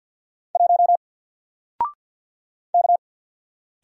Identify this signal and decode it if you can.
Morse code, heard as a keyed tone, about 50 wpm, 700 Hz. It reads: 1 K
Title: Single Character (Letter:Number:Punctuation:Prosign) 50wpm